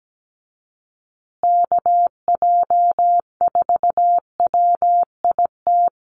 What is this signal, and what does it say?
Morse code, heard as a keyed tone, about 17 wpm, 700 Hz. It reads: KJ4WIT
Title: USA Call Signs Repeated 3x 17wpm